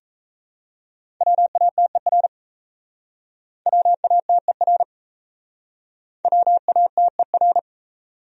Morse code, heard as a keyed tone, about 37 wpm, 700 Hz. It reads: WATER WATER WATER